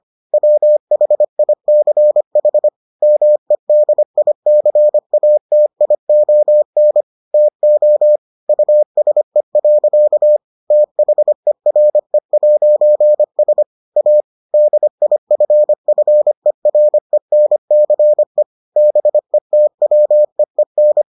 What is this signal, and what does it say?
Morse code, heard as a keyed tone, about 25 wpm, 600 Hz. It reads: WHICH MEDICATION TO USE. THERE'S A DIFFERENCE BETWEEN